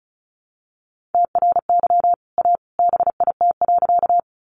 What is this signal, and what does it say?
Morse code, heard as a keyed tone, about 34 wpm, 700 Hz. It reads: TRY A BIT.
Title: Sentences from Top 900 Words 35wpm